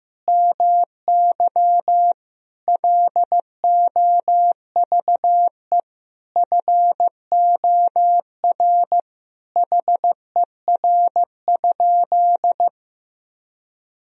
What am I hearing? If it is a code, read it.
Morse code, heard as a keyed tone, about 15 wpm, 700 Hz. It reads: MY LOVE FOR HER?
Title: Sentences from Top 1000 Words - Review Part 1 - 15wpm